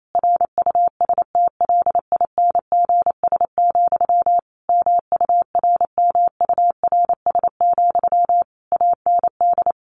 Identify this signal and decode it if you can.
Morse code, heard as a keyed tone, about 28 wpm, 700 Hz. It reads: RUSTLINGS, MURMURS, AND
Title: A Sound Of Thunder 28wpm